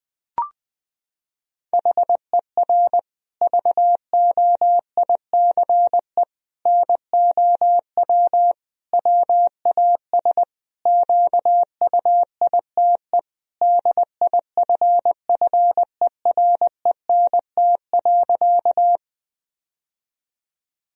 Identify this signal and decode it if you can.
Morse code, heard as a keyed tone, about 20 wpm, 700 Hz. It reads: HER VOICE NOW WAS QUITE DIFFERENT.
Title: Sentences from Top 1000 Words - Review Part 3 - 20wpm